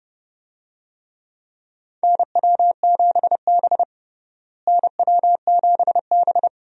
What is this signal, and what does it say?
Morse code, heard as a keyed tone, about 30 wpm, 700 Hz. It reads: NW7B NW7B
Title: USA Call Signs - 4 Characters - Repeated 2x - 30wpm